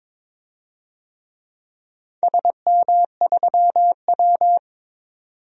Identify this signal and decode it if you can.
Morse code, heard as a keyed tone, about 23 wpm, 700 Hz. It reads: SM3W
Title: DXCC-100 Call Signs - 4 Characters - 22wpm